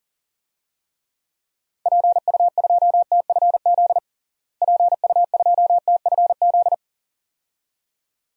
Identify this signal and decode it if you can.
Morse code, heard as a keyed tone, about 40 wpm, 700 Hz. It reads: PU2TFZ PU2TFZ
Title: DXCC-100 Call Signs - 6 Characters - Repeated 2x - 40wpm